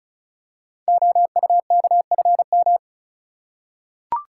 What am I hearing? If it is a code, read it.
Morse code, heard as a keyed tone, about 36 wpm, 700 Hz. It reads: OUKFM
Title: Code Groups - 5 Letters - 35wpm